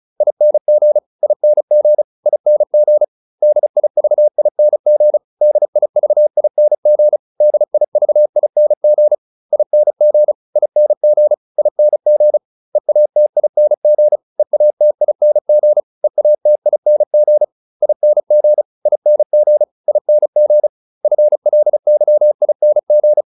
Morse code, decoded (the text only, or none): ING ING ING DIVING DIVING DIVING ING ING ING EATING EATING EATING ING ING ING FLYING